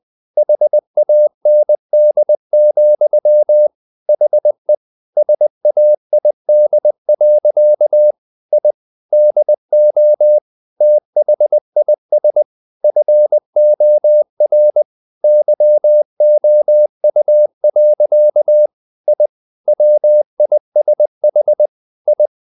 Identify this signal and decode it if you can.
Morse code, heard as a keyed tone, about 20 wpm, 600 Hz. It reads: HAND, HE SAID. I DO THIS FOR YOU. I WISH I
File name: the old man and the sea / the old man and the sea part 1 20wpm